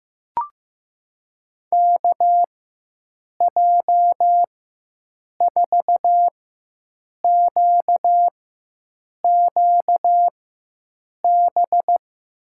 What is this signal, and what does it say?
Morse code, heard as a keyed tone, about 15 wpm, spaced as for 7 wpm, 700 Hz. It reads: KJ4QQB
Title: USA Call Signs - 4x Character Spacing - 15wpm